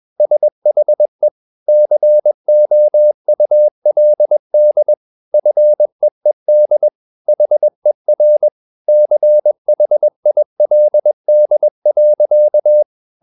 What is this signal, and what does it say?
Morse code, heard as a keyed tone, about 21 wpm, 600 Hz. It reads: SHE COULD FEED HER CHILD.
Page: CW med Gnister